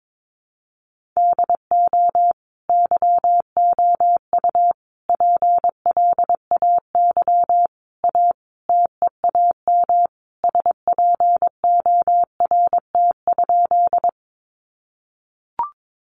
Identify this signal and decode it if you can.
Morse code, heard as a keyed tone, about 22 wpm, 700 Hz. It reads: DO YOU PLAY A TEAM SPORT?
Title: Sentences from Top 1500 Words 22wpm